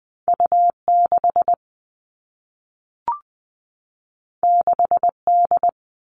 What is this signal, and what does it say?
Morse code, heard as a keyed tone, about 20 wpm, 700 Hz. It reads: U 6 6 D
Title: All Cut Numbers - 2-Digit 20wpm